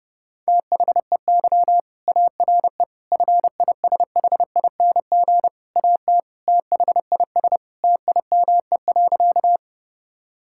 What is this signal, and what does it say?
Morse code, heard as a keyed tone, about 30 wpm, 700 Hz. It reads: THEY ARE FISHING AT THIS TIME.
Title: Sentences from Top 1000 Words - Review Part 5 - 30wpm